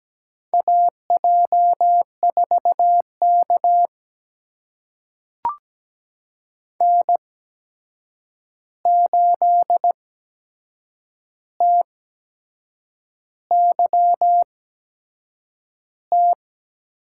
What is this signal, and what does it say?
Morse code, heard as a keyed tone, about 17 wpm, spaced as for 7 wpm, 700 Hz. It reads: AJ4K N8TYT
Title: USA Call Signs - 8x Character Spacing - 17wpm